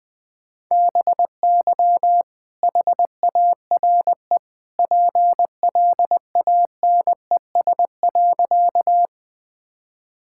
Morse code, heard as a keyed tone, about 21 wpm, 700 Hz. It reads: BY HARE PLANES.
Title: All-Age Appropriate Jokes - 20wpm